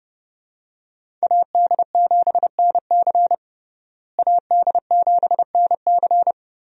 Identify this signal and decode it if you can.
Morse code, heard as a keyed tone, about 32 wpm, 700 Hz. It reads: AD7NC AD7NC
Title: USA Call Signs - 5 Characters - Repeated 3x - 30wpm